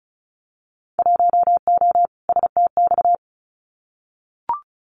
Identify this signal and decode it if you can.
Morse code, heard as a keyed tone, about 34 wpm, 700 Hz. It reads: 1O STX
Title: Winter Field Day Exchanges 35wpm